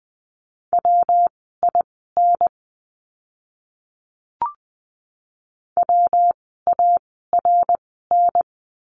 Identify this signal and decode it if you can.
Morse code, heard as a keyed tone, about 20 wpm, spaced as for 15 wpm, 700 Hz. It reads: WIN WARN
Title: 115 Lesson 19f - Letter w - Words 20wpm